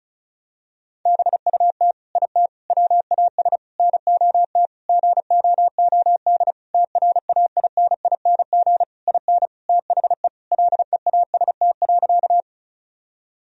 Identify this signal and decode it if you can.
Morse code, heard as a keyed tone, about 36 wpm, 700 Hz. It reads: BUT IT WAS NOT GOOD TRAINING IN THE LEAST.
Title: Sentences from Top 1000 Words - Review Part 4 - 35wpm